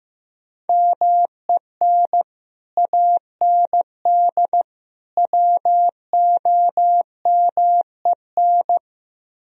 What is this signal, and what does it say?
Morse code, heard as a keyed tone, about 15 wpm, 700 Hz. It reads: MEN AND WOMEN
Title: Speed-Racing - Sets of 3 Words 15wpm